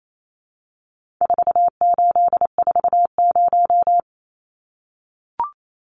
Morse code, heard as a keyed tone, about 28 wpm, 700 Hz. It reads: 4840
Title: Numbers - 4-Digit 28wpm